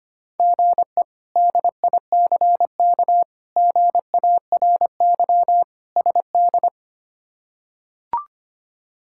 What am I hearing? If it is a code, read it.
Morse code, heard as a keyed tone, about 25 wpm, 700 Hz. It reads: GE DICK GARY SD
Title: K1USN SST QSO Practice - 25wpm